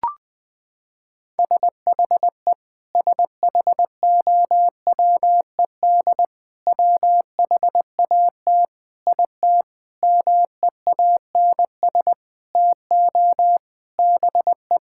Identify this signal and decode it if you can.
Morse code, heard as a keyed tone, about 20 wpm, 700 Hz. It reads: SHE SHOWED WHAT IT MEANS TO BE
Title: Sentences with 6-Word Phrases - 20wpm